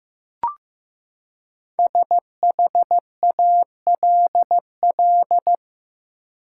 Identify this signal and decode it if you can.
Morse code, heard as a keyed tone, about 15 wpm, 700 Hz. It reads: SHALL
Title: Five Letter Words - 15wpm